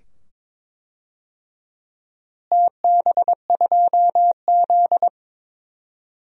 Morse code, heard as a keyed tone, about 22 wpm, 700 Hz. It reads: TB2Z
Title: DXCC-100 Call Signs - 4 Characters - Repeated 2x - 22wpm